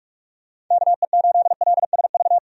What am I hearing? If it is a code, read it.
Morse code, heard as a keyed tone, about 46 wpm, 700 Hz. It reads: KE8LSV
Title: USA Call Signs - 6 Characters - 45wpm